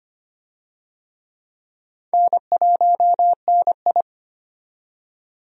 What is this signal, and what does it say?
Morse code, heard as a keyed tone, about 25 wpm, 700 Hz. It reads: N1NI
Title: USA Call Signs - 4 Characters - 25wpm